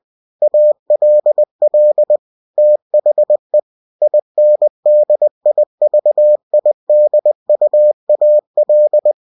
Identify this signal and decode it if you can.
Morse code, heard as a keyed tone, about 20 wpm, 600 Hz. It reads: ALL THE INDIVIDUAL
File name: cw-books / winnie-the-pooh / winnie-20wpm-page-0114